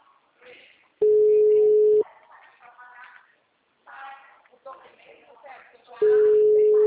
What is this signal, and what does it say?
Ringback tone (Europe, 425 Hz), with speech.